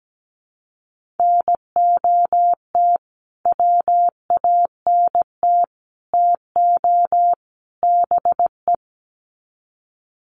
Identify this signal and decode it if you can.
Morse code, heard as a keyed tone, about 17 wpm, 700 Hz. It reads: NOT WANT TO BE
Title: Sets of 4 Words 17wpm